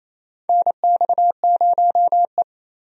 Morse code, heard as a keyed tone, about 27 wpm, 700 Hz. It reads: NX0E
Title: USA Call Signs - 4 Characters - 28wpm